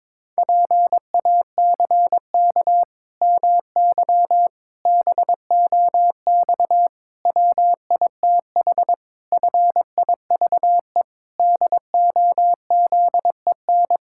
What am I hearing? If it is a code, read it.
Morse code, heard as a keyed tone, about 22 wpm, 700 Hz. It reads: PACK MY BOX WITH FIVE DOZEN
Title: Pangram Challenge v2 - 22wpm